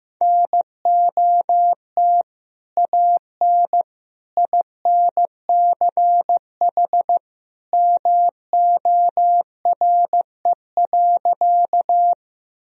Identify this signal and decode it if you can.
Morse code, heard as a keyed tone, about 15 wpm, 700 Hz. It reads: NOT AN INCH MORE.
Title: Sentences from Top 1000 Words - Review Part 3 - 15wpm